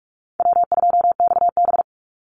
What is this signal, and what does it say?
Morse code, heard as a keyed tone, about 44 wpm, 700 Hz. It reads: W2XB
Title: Easy Call Signs - 45wpm